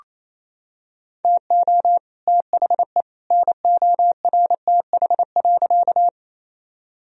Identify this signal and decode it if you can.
Morse code, heard as a keyed tone, about 28 wpm, 700 Hz. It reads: TO THE NORTH.
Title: Sentences from Top 1000 Words - Review Part 1 - 28wpm